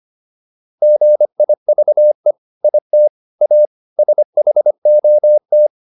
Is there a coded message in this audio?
Yes: Morse code from 0.8 to 5.7 s, heard as a keyed tone, about 25 wpm, 600 Hz. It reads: GIVE IT A SHOT